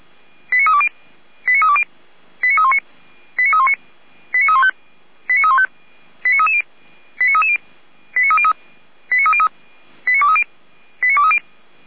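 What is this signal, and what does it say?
Five-tone selective call (ZVEI), heard as quick runs of five tones, tones of about 70 ms. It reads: 89320 89320 89319 89319 89315 89315 89330 89330 89303 89303 89230 89230
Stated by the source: ZVEI five-tone selective call 89110 received with an SDR.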